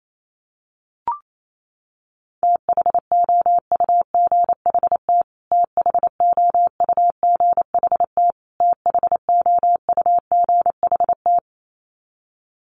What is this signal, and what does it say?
Morse code, heard as a keyed tone, about 28 wpm, 700 Hz. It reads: THOUGHT THOUGHT THOUGHT
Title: Top 500 Words Repeated 28wpm